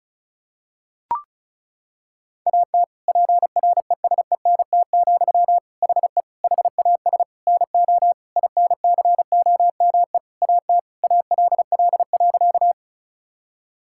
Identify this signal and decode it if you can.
Morse code, heard as a keyed tone, about 36 wpm, 700 Hz. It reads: AT PRESENT, HE HAS NO INCOME AT ALL.
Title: Sentences from Top 1000 Words - Review Part 5 - 35wpm